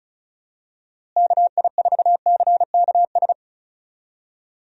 Morse code, heard as a keyed tone, about 36 wpm, 700 Hz. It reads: KI4CKS
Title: USA Call Signs - 6 Characters - Repeated 3x - 35wpm